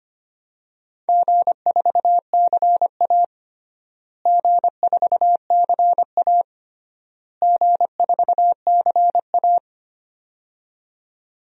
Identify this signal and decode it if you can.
Morse code, heard as a keyed tone, about 25 wpm, 700 Hz. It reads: G4CA G4CA G4CA
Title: DXCC-100 Call Signs - 4 Characters - Repeated 3x - 25wpm